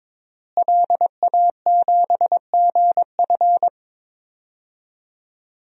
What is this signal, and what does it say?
Morse code, heard as a keyed tone, about 21 wpm, 700 Hz. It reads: LA7GF
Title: DXCC-100 Call Signs - 5 Characters - Repeated 3x - 22wpm